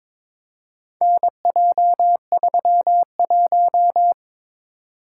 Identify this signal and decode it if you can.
Morse code, heard as a keyed tone, about 22 wpm, 700 Hz. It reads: NJ31
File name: practice / 22wpm / Four Letter-Number Combinations - 22wpm